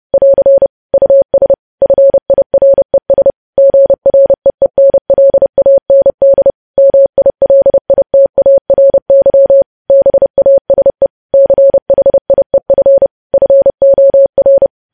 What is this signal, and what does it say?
Morse code, heard as a keyed tone, about 30 wpm, 550 Hz. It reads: <AR> US FIRES GREENLAND MILITARY BASE CHIEF FOR
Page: news headlines in morse code at 30 wpm